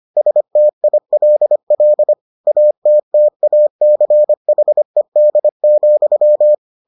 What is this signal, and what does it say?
Morse code, heard as a keyed tone, about 25 wpm, 600 Hz. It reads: STILL ATTACHED,